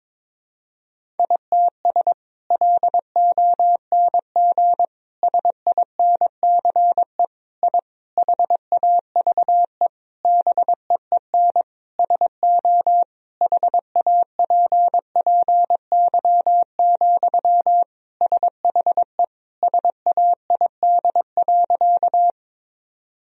Morse code, heard as a keyed tone, about 22 wpm, 700 Hz. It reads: ITS LONG SINCE I HAVE BEEN SO HAPPY, SHE SAID.
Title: Sentences from Top 1000 Words - Review Part 1 - 22wpm